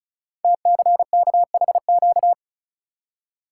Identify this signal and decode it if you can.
Morse code, heard as a keyed tone, about 36 wpm, 690 Hz. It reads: TCKHQ
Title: Code Groups - 5 Letters - 35wpm